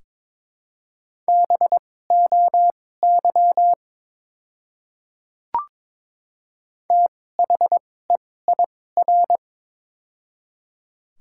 Morse code, heard as a keyed tone, about 21 wpm, spaced as for 16 wpm, 700 Hz. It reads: BOY THEIR
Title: 2x Character Spacing - Top 200 Words - 22wpm